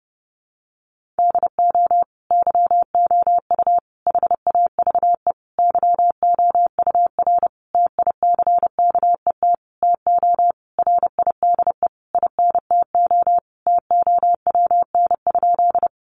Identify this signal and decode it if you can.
Morse code, heard as a keyed tone, about 30 wpm, 700 Hz. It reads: DO YOU HAVE YOUR TICKET TO RIDE INTO TOWN?
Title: Sentences from Top 1400 Words 30wpm